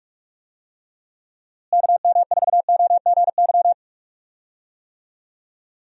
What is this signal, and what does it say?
Morse code, heard as a keyed tone, about 41 wpm, 700 Hz. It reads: KM4OGY
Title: USA Call Signs - 6 Characters - Repeated 3x - 45wpm